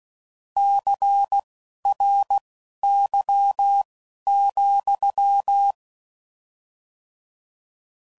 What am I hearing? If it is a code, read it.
Morse code, heard as a keyed tone, about 16 wpm, spaced as for 11 wpm, 800 Hz. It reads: CRY,